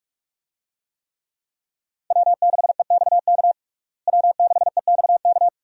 Morse code, heard as a keyed tone, about 46 wpm, 700 Hz. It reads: W6EXK W6EXK
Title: USA Call Signs - 5 Characters - Repeated 2x - 45wpm